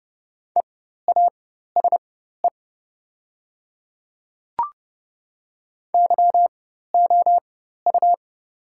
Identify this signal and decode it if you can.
Morse code, heard as a keyed tone, about 30 wpm, spaced as for 14 wpm, 700 Hz. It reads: EASE YOU
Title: 4x Character Spacing - Top 300 Words - 30wpm